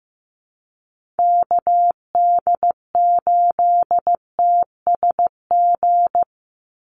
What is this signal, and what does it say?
Morse code, heard as a keyed tone, about 15 wpm, 700 Hz. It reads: KD8TSG
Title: USA Call Signs 15wpm